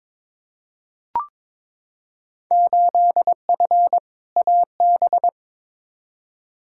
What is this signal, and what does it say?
Morse code, heard as a keyed tone, about 22 wpm, 700 Hz. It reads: 8F AB
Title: Field Day Exchanges Encore - 22wpm